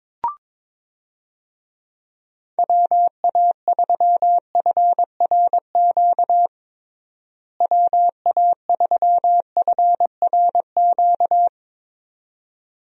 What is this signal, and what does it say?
Morse code, heard as a keyed tone, about 22 wpm, 700 Hz. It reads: WA3FRQ WA3FRQ
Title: USA Call Signs - 6 Characters - Repeated 2x - 22wpm